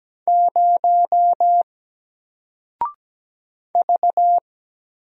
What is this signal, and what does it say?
Morse code, heard as a keyed tone, about 17 wpm, 700 Hz. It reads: 0 V
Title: Single Letter-Number 17wpm